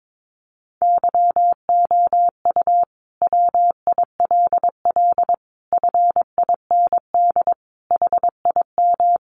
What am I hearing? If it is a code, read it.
Morse code, heard as a keyed tone, about 22 wpm, 700 Hz. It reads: YOU WILL FIND HIM
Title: Sentences from Top 1100 Words 22wpm